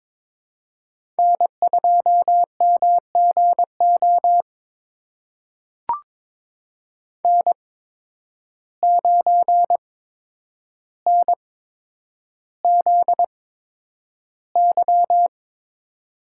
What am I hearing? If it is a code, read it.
Morse code, heard as a keyed tone, about 23 wpm, spaced as for 10 wpm, 700 Hz. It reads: N2MGO N9NZY
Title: USA Call Signs - 8x Character Spacing - 22wpm